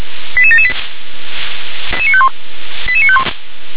Five-tone selective call (ZVEI), heard as quick runs of five tones, tones of about 70 ms. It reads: 88778 88951
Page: SDR/ZVEI